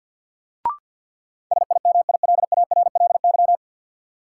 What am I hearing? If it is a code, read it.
Morse code, heard as a keyed tone, about 53 wpm, 700 Hz. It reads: SIMILARLY
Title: Sentences from Top 1400 Words - New Word Drill Alt - 50wpm